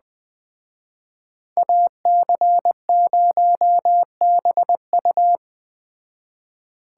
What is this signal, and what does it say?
Morse code, heard as a keyed tone, about 20 wpm, 700 Hz. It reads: AC0BU
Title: Speed Racing - USA 2x2 Call Signs 17wpm